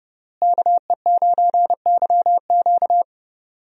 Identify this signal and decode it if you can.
Morse code, heard as a keyed tone, about 31 wpm, 700 Hz. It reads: KE9YQ
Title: USA Call Signs - 5 Characters - Repeated 2x - 30wpm